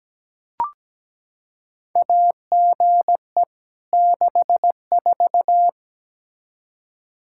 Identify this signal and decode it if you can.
Morse code, heard as a keyed tone, about 17 wpm, 700 Hz. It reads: AGE 64
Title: Practice QSO - Exchange 3 - More Info - 17wpm